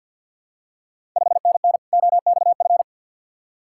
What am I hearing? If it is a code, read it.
Morse code, heard as a keyed tone, about 47 wpm, 690 Hz. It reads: 5NN OXF
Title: Ontario QSO Party Exchange 50wpm